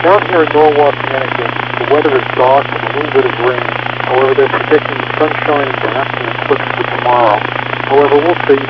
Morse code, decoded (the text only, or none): none